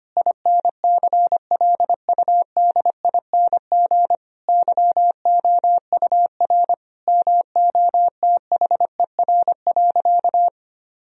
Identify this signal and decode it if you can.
Morse code, heard as a keyed tone, about 25 wpm, 700 Hz. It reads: INCLUDING YOUR MOTHER.
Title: Sentences from Top 1000 Words - Review Part 4 - 25wpm